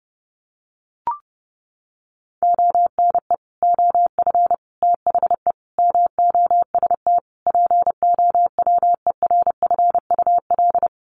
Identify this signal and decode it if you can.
Morse code, heard as a keyed tone, about 30 wpm, 700 Hz. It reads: ONE OF THE MOST POWERFUL
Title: Sets of 5 Words Encore 30wpm